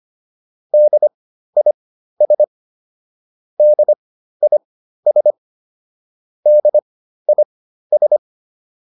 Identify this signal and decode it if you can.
Morse code, heard as a keyed tone, about 25 wpm, spaced as for 13 wpm, 600 Hz. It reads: DIS DIS DIS